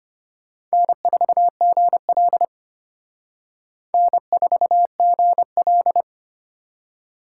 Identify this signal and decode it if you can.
Morse code, heard as a keyed tone, about 32 wpm, 700 Hz. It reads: N4GL N4GL